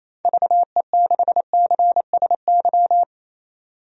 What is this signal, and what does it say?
Morse code, heard as a keyed tone, about 27 wpm, 700 Hz. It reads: VE6CSY
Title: DXCC-100 Call Signs - 6 Characters - 28wpm